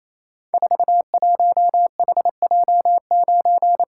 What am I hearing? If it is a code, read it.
Morse code, heard as a keyed tone, about 27 wpm, 700 Hz. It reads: 41HJ9
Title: Code Groups - 5 Letter:Numbers - 28wpm